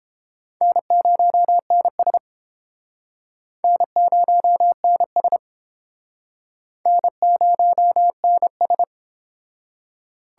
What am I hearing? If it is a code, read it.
Morse code, heard as a keyed tone, about 29 wpm, 700 Hz. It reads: N0NS N0NS N0NS